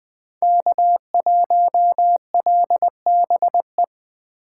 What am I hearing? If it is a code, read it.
Morse code, heard as a keyed tone, about 20 wpm, 700 Hz. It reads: K1LBE
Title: Speed Racing - USA Call Signs 17wpm